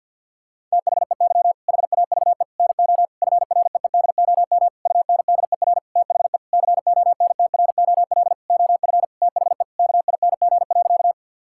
Morse code, heard as a keyed tone, about 50 wpm, 700 Hz. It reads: THEY HAVE NO FREEDOM UNDER THE CONTROL OF THE KING.
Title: Sentences from Top 1000 Words - Review Part 5 - 50wpm